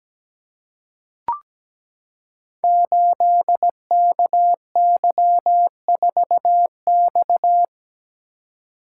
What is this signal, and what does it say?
Morse code, heard as a keyed tone, about 17 wpm, 700 Hz. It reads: 8KY4X
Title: Code Groups - 5 Letter:Numbers - 17wpm